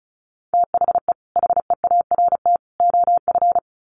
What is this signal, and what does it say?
Morse code, heard as a keyed tone, about 34 wpm, 700 Hz. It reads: THE HEART OF
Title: Sets of 3 Words 35wpm